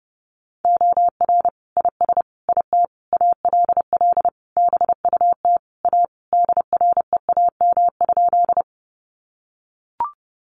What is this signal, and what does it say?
Morse code, heard as a keyed tone, about 30 wpm, 700 Hz. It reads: OR IS IT ALL BUT A DREAM?
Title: Sentences from Top 1500 Words 30wpm